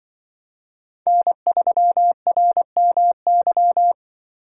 Morse code, heard as a keyed tone, about 24 wpm, 700 Hz. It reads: N3RMY
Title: Speed Racing - USA 1x3 Call Signs 20wpm